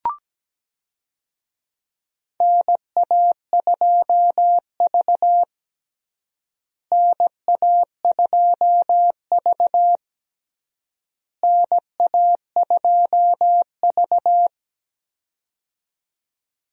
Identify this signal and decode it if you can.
Morse code, heard as a keyed tone, about 17 wpm, 700 Hz. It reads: NA2V NA2V NA2V
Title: USA Call Signs - 4 Characters - Repeated 3x - 17wpm